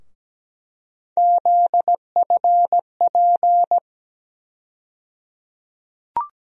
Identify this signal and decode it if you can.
Morse code, heard as a keyed tone, about 17 wpm, 700 Hz. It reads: ZFP
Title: USA Call Sign Postfixes - 3 Letters 17wpm